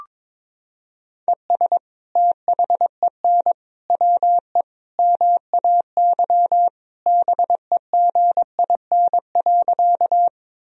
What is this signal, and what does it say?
Morse code, heard as a keyed tone, about 21 wpm, 700 Hz. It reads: ES THEN WE MAY BEGIN.
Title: Sentences from Top 300 Words and Abbreviations 22wpm